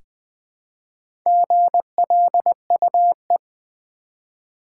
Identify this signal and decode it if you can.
Morse code, heard as a keyed tone, about 20 wpm, 700 Hz. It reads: GLUE
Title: Easy Words with Hints - 20wpm